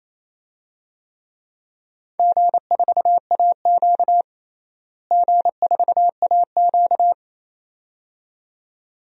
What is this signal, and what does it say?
Morse code, heard as a keyed tone, about 28 wpm, 700 Hz. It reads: G4AQ G4AQ
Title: DXCC-100 Call Signs - 4 Characters - Repeated 2x - 28wpm